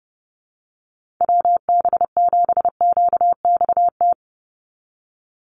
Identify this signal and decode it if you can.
Morse code, heard as a keyed tone, about 30 wpm, 700 Hz. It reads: WB7QXT
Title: USA Call Signs 30wpm